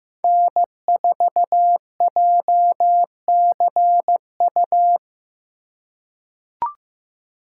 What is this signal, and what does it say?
Morse code, heard as a keyed tone, about 15 wpm, 700 Hz. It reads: N4JCU
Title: USA Call Signs - 5 Characters - 4x Character Spacing - 15wpm